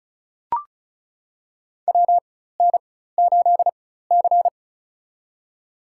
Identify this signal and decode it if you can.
Morse code, heard as a keyed tone, about 36 wpm, spaced as for 16 wpm, 700 Hz. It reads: WN8C
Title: USA Call Signs - 4 Characters - 4x Character Spacing - 35wpm